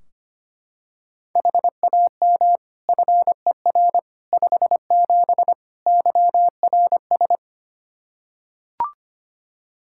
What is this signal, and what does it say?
Morse code, heard as a keyed tone, about 25 wpm, 700 Hz. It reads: HAM FER 57 YRS
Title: K1USN SST QSO Practice 25wpm